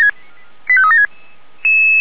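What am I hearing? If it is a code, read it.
Five-tone selective call (ZVEI), heard as quick runs of five tones, tones of about 70 ms. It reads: 85376
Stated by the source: ZVEI five-tone call 89485 received by SDR.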